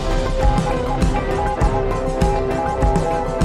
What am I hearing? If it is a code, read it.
Morse code, heard as a keyed tone, about 21 wpm, 820 Hz. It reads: NEED TO